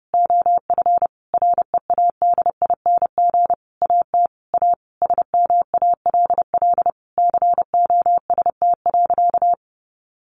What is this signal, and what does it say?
Morse code, heard as a keyed tone, about 28 wpm, 700 Hz. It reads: OF READING AT A SMALL COST.
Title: Sentences from Top 900 Words 30wpm